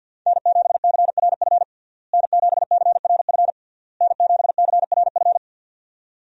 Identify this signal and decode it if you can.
Morse code, heard as a keyed tone, about 51 wpm, 700 Hz. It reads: N7CRF N7CRF N7CRF
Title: USA Call Signs - 5 Characters - Repeated 3x - 50wpm